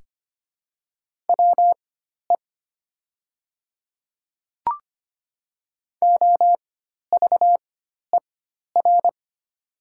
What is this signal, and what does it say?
Morse code, heard as a keyed tone, about 25 wpm, spaced as for 12 wpm, 700 Hz. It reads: WE OVER
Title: 4x Character Spacing - Top 200 Words - 25wpm